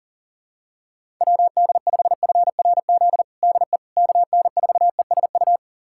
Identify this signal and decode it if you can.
Morse code, heard as a keyed tone, about 40 wpm, 700 Hz. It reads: WD5FRZ DE KN4ESU
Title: USA Callsign DE Callsign 40wpm